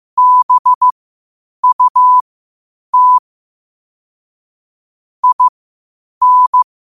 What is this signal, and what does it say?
Morse code, heard as a keyed tone, about 15 wpm, spaced as for 8 wpm, 1000 Hz. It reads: BUT IN
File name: qotdfiles / 8wpm